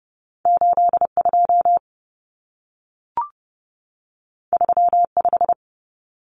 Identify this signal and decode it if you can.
Morse code, heard as a keyed tone, about 30 wpm, 700 Hz. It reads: 82 35
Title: Common Cut Numbers - 2-Digit 30wpm